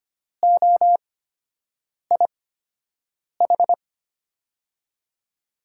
Morse code, heard as a keyed tone, about 25 wpm, spaced as for 7 wpm, 700 Hz. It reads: OIH
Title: USA Call Signs - 5 Characters - 8x Character Spacing - 25wpm